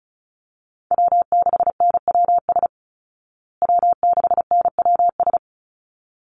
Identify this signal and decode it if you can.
Morse code, heard as a keyed tone, about 34 wpm, 700 Hz. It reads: W6NWS W6NWS
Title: Call Signs Repeated 3x - 35wpm